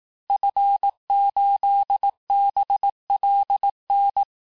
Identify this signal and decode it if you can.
Morse code, heard as a keyed tone, about 18 wpm, 800 Hz. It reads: F8BLN
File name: cours-du-lundi files / cours-1594--FIN-de-QSO-vitesse-17mpm